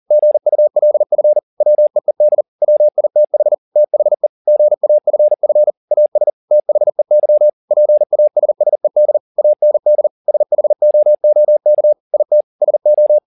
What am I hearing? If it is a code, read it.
Morse code, heard as a keyed tone, about 40 wpm, 600 Hz. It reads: GULF WEED WITH THE GAFF AS THEY PASSED AND SHOOK IT SO
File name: the old man and the sea 2 40wpm